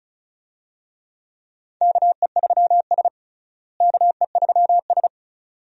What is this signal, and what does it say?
Morse code, heard as a keyed tone, about 36 wpm, 700 Hz. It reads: KE3S KE3S
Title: USA Call Signs - 4 Characters - Repeated 2x - 35wpm